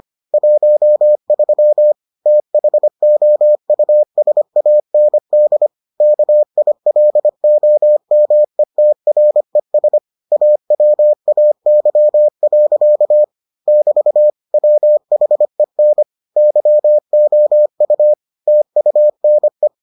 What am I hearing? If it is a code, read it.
Morse code, heard as a keyed tone, about 25 wpm, 600 Hz. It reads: 13 THOUSAND KILOMETRES AWAY. <BT> WHEN YOU TUNE